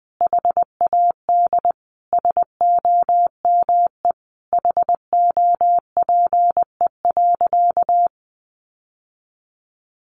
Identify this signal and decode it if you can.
Morse code, heard as a keyed tone, about 20 wpm, 700 Hz. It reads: HAD SOME HOPE.
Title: Sentences from Top 1000 Words 20wpm